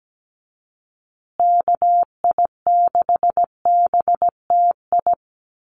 Morse code, heard as a keyed tone, about 17 wpm, 700 Hz. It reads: KI6BTI
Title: USA Call Signs Repeated 2x 17wpm